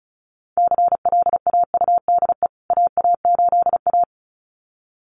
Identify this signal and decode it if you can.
Morse code, heard as a keyed tone, about 34 wpm, 700 Hz. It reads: CLAUDE AA8A
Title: CWT Exchange 35wpm